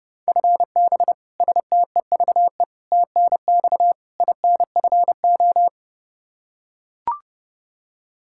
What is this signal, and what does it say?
Morse code, heard as a keyed tone, about 30 wpm, 700 Hz. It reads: FB STEVE TNX INFO
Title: Practice QSO - Exchange 3 - More Info - 30wpm